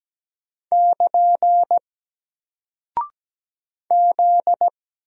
Morse code, heard as a keyed tone, about 17 wpm, 700 Hz. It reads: <KN> Z
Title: Single Character (Letter:Number:Punctuation:Prosign) Alt 17wpm